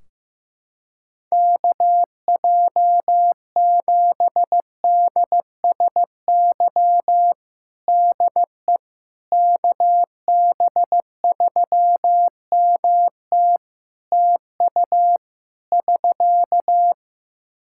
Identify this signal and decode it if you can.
Morse code, heard as a keyed tone, about 15 wpm, 700 Hz. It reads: KJ7DSY DE KB3MT TU <SK>
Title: K1USN SST QSO Practice 15wpm